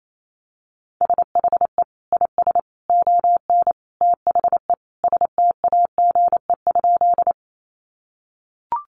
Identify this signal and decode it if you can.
Morse code, heard as a keyed tone, about 28 wpm, 700 Hz. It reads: SHE IS ON THE STAGE?
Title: Sentences from Top 1200 Words 28wpm